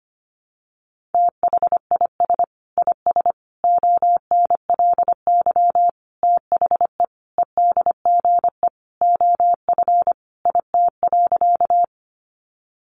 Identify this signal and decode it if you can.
Morse code, heard as a keyed tone, about 25 wpm, 700 Hz. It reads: THIS IS ONLY THE EDGE OF IT.
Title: Sentences from Top 1300 Words 25wpm